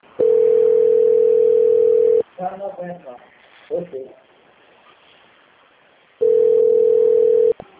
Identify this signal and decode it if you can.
Ringback tone (North America), 440 + 480 Hz, with speech.
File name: out- - - - - .47297